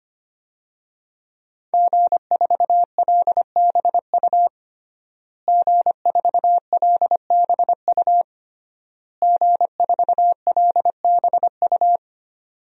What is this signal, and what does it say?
Morse code, heard as a keyed tone, about 25 wpm, 700 Hz. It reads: G4LBU G4LBU G4LBU
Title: DXCC-100 Call Signs - 5 Characters - Repeated 3x - 25wpm